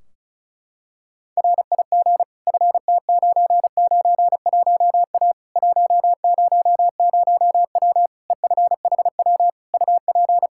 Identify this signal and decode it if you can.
Morse code, heard as a keyed tone, about 36 wpm, 700 Hz. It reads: RIG FT991A 100W EFHW UP
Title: Practice QSO - Exchange 2 - Info - 35wpm